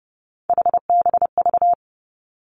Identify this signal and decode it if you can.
Morse code, heard as a keyed tone, about 30 wpm, 700 Hz. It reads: HBV
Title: Three Challenging Letter-Number Combinations 30wpm